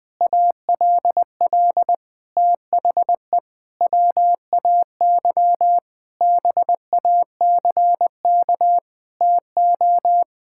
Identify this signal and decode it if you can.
Morse code, heard as a keyed tone, about 20 wpm, 700 Hz. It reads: ALL THE WAY BACK TO
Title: Sentences with 6-Word Phrases - 20wpm